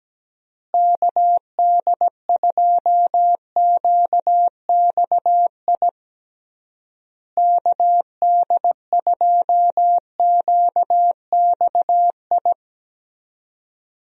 Morse code, heard as a keyed tone, about 17 wpm, 700 Hz. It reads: KD2QXI KD2QXI
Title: USA Call Signs - 6 Characters - Repeated 3x - 17wpm